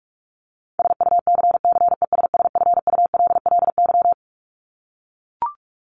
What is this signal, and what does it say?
Morse code, heard as a keyed tone, about 44 wpm, 700 Hz. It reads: SUCCESSFULLY
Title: Top 1501-1750 Words 45wpm